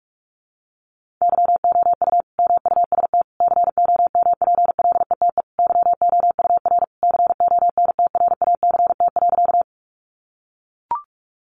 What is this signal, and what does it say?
Morse code, heard as a keyed tone, about 44 wpm, 700 Hz. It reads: YOU MUST COMPLETE YOUR CONTRACT.
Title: Sentences from Top 1400 Words 45wpm